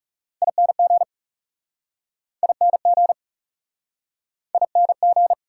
Morse code, heard as a keyed tone, about 42 wpm, 700 Hz. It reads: ING ING ING